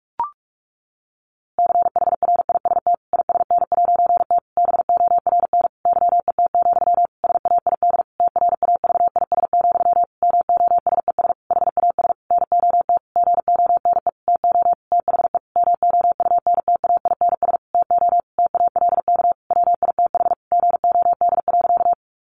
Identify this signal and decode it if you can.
Morse code, heard as a keyed tone, about 44 wpm, 700 Hz. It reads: CHRIST ISN'T BORN YET, SAID TRAVIS, MOSES HAS NOT GONE TO THE MOUNTAINS TO TALK WITH GOD.
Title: A Sound Of Thunder 45wpm